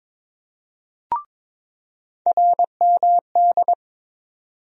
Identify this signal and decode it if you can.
Morse code, heard as a keyed tone, about 22 wpm, 700 Hz. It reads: RMD
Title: USA Call Sign Postfixes - 3 Letters 22wpm